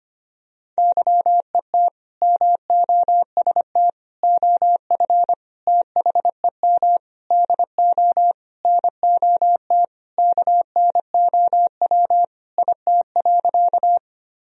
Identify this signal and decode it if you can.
Morse code, heard as a keyed tone, about 25 wpm, 700 Hz. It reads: YET MOST OF THEM DO NOT KNOW IT.
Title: Sentences from Top 1000 Words - Review Part 1 - 25wpm